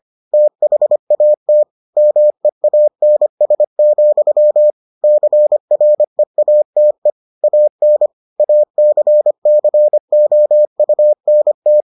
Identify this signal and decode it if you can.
Morse code, heard as a keyed tone, about 25 wpm, 600 Hz. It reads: THAT MEANS, CREATE AN ACCOUNT